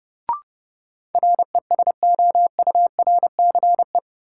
Morse code, heard as a keyed tone, about 29 wpm, 700 Hz. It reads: RESOURCE
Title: Sentences from Top 1300 Words - New Word Drill Alt - 30wpm